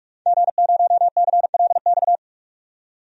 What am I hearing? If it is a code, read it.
Morse code, heard as a keyed tone, about 46 wpm, 700 Hz. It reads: G0CLX
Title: DXCC-100 Call Signs - 5 Characters - 45wpm